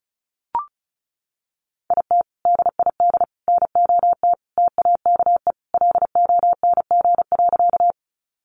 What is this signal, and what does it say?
Morse code, heard as a keyed tone, about 34 wpm, 700 Hz. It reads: IT DID NOT TAKE LONG.
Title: Sentences from Top 600 Words 35wpm